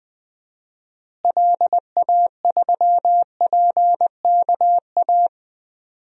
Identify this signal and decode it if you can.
Morse code, heard as a keyed tone, about 20 wpm, 700 Hz. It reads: LA3PKA
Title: DXCC-100 Call Signs - 6 Characters - Repeated 2x - 20wpm